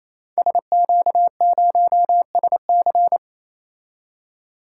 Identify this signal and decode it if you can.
Morse code, heard as a keyed tone, about 29 wpm, 700 Hz. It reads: SQ0SC
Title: Code Groups - 5 Letter:Numbers - 28wpm